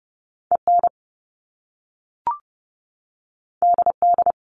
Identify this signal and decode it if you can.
Morse code, heard as a keyed tone, about 30 wpm, 700 Hz. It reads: EN DD
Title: Top Two Letter Combinations in Words 30wpm